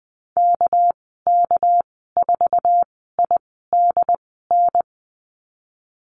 Morse code, heard as a keyed tone, about 20 wpm, spaced as for 15 wpm, 700 Hz. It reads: KK4IDN
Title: 219 Lesson 35g - Number 0 - Call Signs 20wpm